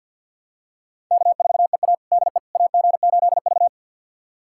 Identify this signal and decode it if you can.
Morse code, heard as a keyed tone, about 48 wpm, 700 Hz. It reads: K4EA DE AG8V
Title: USA Callsign DE Callsign - 4 Characters - 50wpm